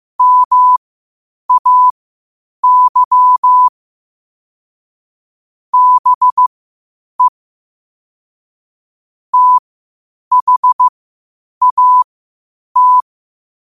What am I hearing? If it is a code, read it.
Morse code, heard as a keyed tone, about 15 wpm, spaced as for 8 wpm, 1000 Hz. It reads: MAY BE THAT
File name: qotdfiles / 8wpm